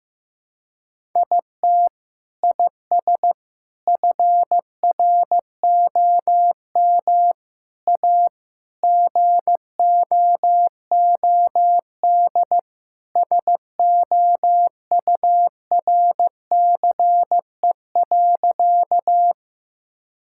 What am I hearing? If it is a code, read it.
Morse code, heard as a keyed tone, about 15 wpm, 700 Hz. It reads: IT IS FROM A GOOD SOURCE.
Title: Sentences from Top 1000 Words - Review Part 5 - 15wpm